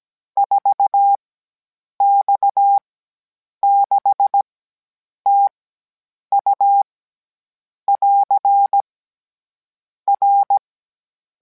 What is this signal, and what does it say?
Morse code, heard as a keyed tone, about 17 wpm, spaced as for 8 wpm, 800 Hz. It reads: 4X6TU <AR> R